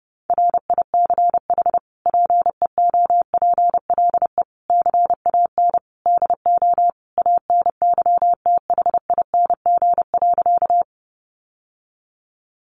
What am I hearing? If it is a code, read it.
Morse code, heard as a keyed tone, about 30 wpm, 700 Hz. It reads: RICH PEOPLE CAN DO ANYTHING.
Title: Sentences from Top 1100 Words 30wpm